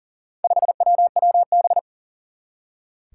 Morse code, heard as a keyed tone, about 40 wpm, 690 Hz. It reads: 5WWB
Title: Four Challenging Letter-Number Combinations 40wpm